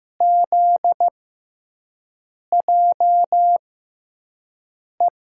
Morse code, heard as a keyed tone, about 15 wpm, spaced as for 5 wpm, 700 Hz. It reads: ZJE
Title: USA Call Signs - 5 Characters - 6x Character Spacing - 15wpm